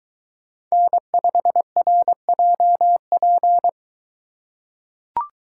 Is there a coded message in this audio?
Yes: Morse code from 0.7 to 3.7 s, heard as a keyed tone, about 23 wpm, 700 Hz. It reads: N5RJP